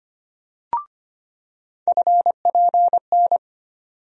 Morse code, heard as a keyed tone, about 25 wpm, 700 Hz. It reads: FPN
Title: USA Call Sign Postfixes - 3 Letters 25wpm